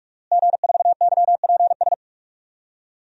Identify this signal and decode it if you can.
Morse code, heard as a keyed tone, about 46 wpm, 700 Hz. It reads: G4YPS
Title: DXCC-100 Call Signs - 5 Characters - Repeated 2x - 45wpm